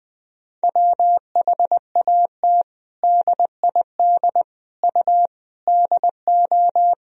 Morse code, heard as a keyed tone, about 20 wpm, 700 Hz. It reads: WHAT DID U DO
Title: Sentences from Top 100 Words and Abbreviations 20wpm